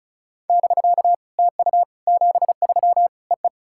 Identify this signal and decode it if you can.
Morse code, heard as a keyed tone, about 36 wpm, 700 Hz. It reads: <BK> TU 73 EE
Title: SOTA QSO Practice 35wpm